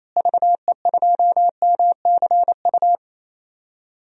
Morse code, heard as a keyed tone, about 28 wpm, 700 Hz. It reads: VE2MCU
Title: DXCC-100 Call Signs - 6 Characters - Repeated 3x - 28wpm